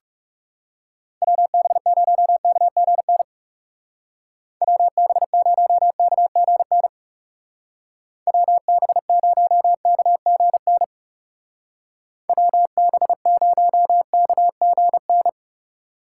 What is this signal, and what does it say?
Morse code, heard as a keyed tone, about 37 wpm, 700 Hz. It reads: WB0KGN WB0KGN WB0KGN WB0KGN
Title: Speed Racing - USA Call Signs 30wpm